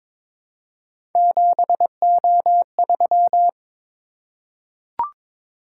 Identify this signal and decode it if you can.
Morse code, heard as a keyed tone, about 23 wpm, 700 Hz. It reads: 7O3
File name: practice / 22wpm / Three Letter-Number Combinations - 22wpm